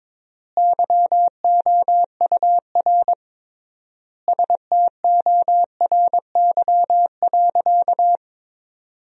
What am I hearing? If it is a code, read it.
Morse code, heard as a keyed tone, about 22 wpm, 700 Hz. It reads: YOUR STORY.
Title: 3x Word-Spacing - Sentences from Top 300 Words - 22wpm